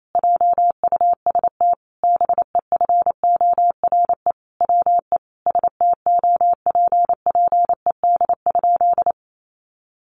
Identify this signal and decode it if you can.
Morse code, heard as a keyed tone, about 28 wpm, 700 Hz. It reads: JUST BEFORE WE STOPPED?
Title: A Sound Of Thunder 28wpm